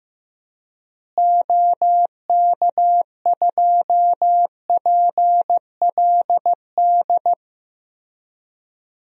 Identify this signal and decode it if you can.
Morse code, heard as a keyed tone, about 15 wpm, 700 Hz. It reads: OK2PLD
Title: DXCC-100 Call Signs - 6 Characters - Repeated 3x - 15wpm